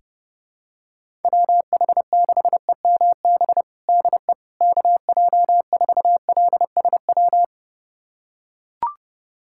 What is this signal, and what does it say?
Morse code, heard as a keyed tone, about 30 wpm, 700 Hz. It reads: WH6EMB DE KJ4LSW
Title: USA Callsign DE Callsign 30wpm